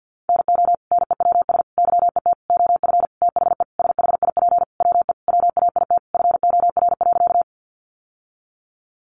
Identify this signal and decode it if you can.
Morse code, heard as a keyed tone, about 50 wpm, 700 Hz. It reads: NO NEWS YET OF THE SHIP WE WAIT FOR.
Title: Sentences from Top 800 Words 50wpm